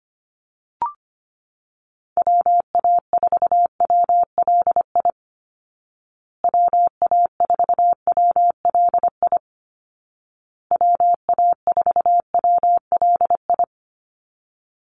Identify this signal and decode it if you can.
Morse code, heard as a keyed tone, about 25 wpm, 700 Hz. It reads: WA4WLI WA4WLI WA4WLI
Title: Call Signs Repeated 2x - 25wpm